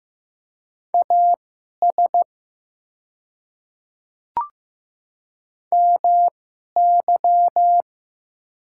Morse code, heard as a keyed tone, about 15 wpm, spaced as for 11 wpm, 700 Hz. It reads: AS MY
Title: 2x Character Spacing - Top 100 Words - 15wpm